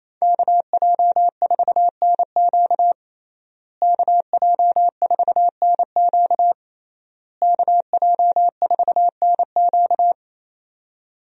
Morse code, heard as a keyed tone, about 28 wpm, 700 Hz. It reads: KJ4NQ KJ4NQ KJ4NQ
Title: USA Call Signs - 5 Characters - Repeated 3x - 28wpm